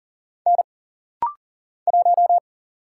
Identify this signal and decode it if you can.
Morse code, heard as a keyed tone, about 39 wpm, 700 Hz. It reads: N 1
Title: Single Letter-Number Alt - Rapid-Fire 40wpm